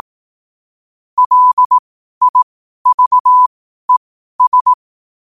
Morse code, heard as a keyed tone, about 18 wpm, spaced as for 12 wpm, 1000 Hz. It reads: LIVES
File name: 12wpm